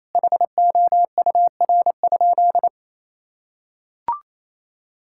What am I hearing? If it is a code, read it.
Morse code, heard as a keyed tone, about 28 wpm, 700 Hz. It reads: HOUR?
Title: Sentences from Top 1000 Words - Review Part 5 - 28wpm